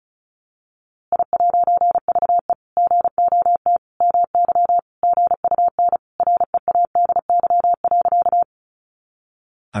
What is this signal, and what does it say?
Morse code, heard as a keyed tone, about 34 wpm, 700 Hz. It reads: I'VE GOT MY GUN READY.
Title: A Sound Of Thunder 35wpm